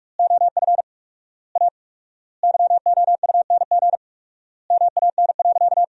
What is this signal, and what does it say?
Morse code, heard as a keyed tone, about 45 wpm, 700 Hz. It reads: OF A YOUNG MAN.
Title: 4x Word-Spacing - Sentences from Top 500 Words - 45wpm